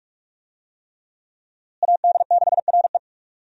Morse code, heard as a keyed tone, about 43 wpm, 700 Hz. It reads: AD6RE
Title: USA Call Signs - 5 Characters - Repeated 3x - 45wpm